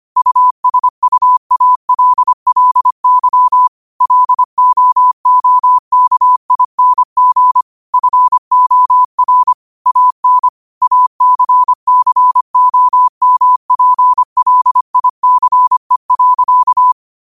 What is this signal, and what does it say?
Morse code, heard as a keyed tone, about 25 wpm, 1000 Hz. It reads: USUALLY LOOKING FOR AN ACCOMPLICE.
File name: qotdfiles / 25wpm